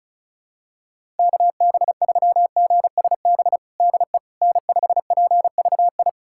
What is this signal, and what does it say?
Morse code, heard as a keyed tone, about 36 wpm, 700 Hz. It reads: KB3GSB DE N5PVI
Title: USA Callsign DE Callsign 35wpm